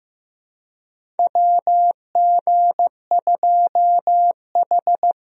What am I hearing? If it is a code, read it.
Morse code, heard as a keyed tone, about 15 wpm, 700 Hz. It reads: WG2H
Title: USA Call Signs - 4 Characters - Repeated 2x - 15wpm